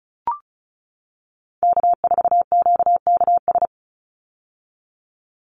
Morse code, heard as a keyed tone, about 34 wpm, 700 Hz. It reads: K4QKS
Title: USA Call Signs 35wpm